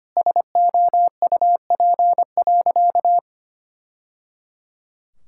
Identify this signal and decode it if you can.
Morse code, heard as a keyed tone, about 25 wpm, 700 Hz. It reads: SOUP.
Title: Sentences from Top 1000 Words - Review Part 6 - 25wpm